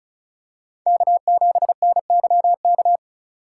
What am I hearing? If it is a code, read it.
Morse code, heard as a keyed tone, about 36 wpm, 700 Hz. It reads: K7NYK
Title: USA Call Signs - 3x Character Spacing - 35wpm